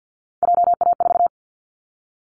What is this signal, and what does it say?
Morse code, heard as a keyed tone, about 50 wpm, 700 Hz. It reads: JA4
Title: Callsign Prefixes for Top 25 DXCC - 50wpm